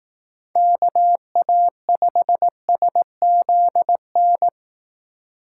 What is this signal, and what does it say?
Morse code, heard as a keyed tone, about 18 wpm, 700 Hz. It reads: KA5SZN
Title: Speed Racing - USA 2x3 Call Signs 15wpm